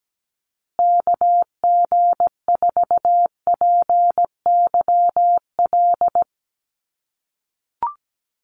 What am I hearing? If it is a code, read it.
Morse code, heard as a keyed tone, about 17 wpm, 700 Hz. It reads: KG4PYL
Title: USA Call Signs Repeated 2x 17wpm